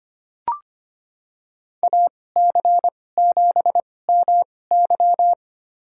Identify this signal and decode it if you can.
Morse code, heard as a keyed tone, about 25 wpm, spaced as for 18 wpm, 700 Hz. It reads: AC7MY
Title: USA Call Signs - 2x Character Spacing - 25wpm